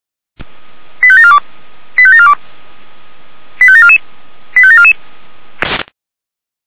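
Five-tone selective call (ZVEI), heard as quick runs of five tones, tones of about 70 ms. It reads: 85742 85742 85744 85744
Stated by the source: Five-tone ZVEI selective call received on an SDR.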